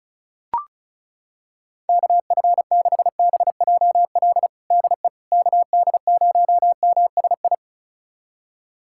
Morse code, heard as a keyed tone, about 34 wpm, 700 Hz. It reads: KF6BJL DE KD0MSI
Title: USA Callsign DE Callsign - 6 Characters - 35wpm